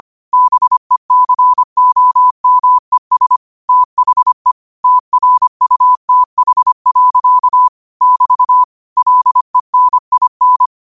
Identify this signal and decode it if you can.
Morse code, heard as a keyed tone, about 25 wpm, 1000 Hz. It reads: BECOMES THE TRUTH. <BT> LENIN
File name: qotdfiles / 25wpm